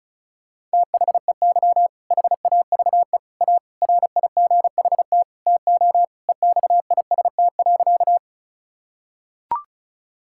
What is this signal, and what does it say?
Morse code, heard as a keyed tone, about 34 wpm, 690 Hz. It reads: THEY HAVE A RIGHT TO EXIST.
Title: Sentences from Top 1000 Words - Review Part 5 - 35wpm